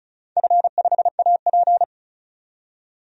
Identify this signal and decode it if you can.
Morse code, heard as a keyed tone, about 34 wpm, 700 Hz. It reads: F5AP
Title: DXCC-100 Call Signs - 4 Characters - 35wpm